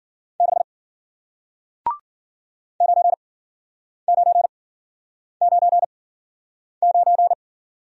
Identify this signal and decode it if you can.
Morse code, heard as a keyed tone, about 53 wpm, spaced as for 9 wpm, 700 Hz. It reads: 6 9999